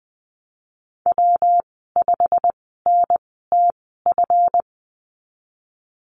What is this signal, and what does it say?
Morse code, heard as a keyed tone, about 19 wpm, spaced as for 14 wpm, 700 Hz. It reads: W5NTF
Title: 213 Lesson 34g - Number 8 - Call Signs 20wpm